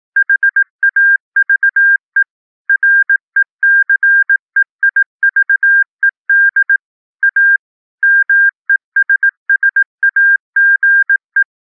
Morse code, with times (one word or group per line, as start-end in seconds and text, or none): 0.2-2.2 HAVE
2.7-6.8 RECEIVED
7.2-7.6 A
8.0-11.4 MESSAGE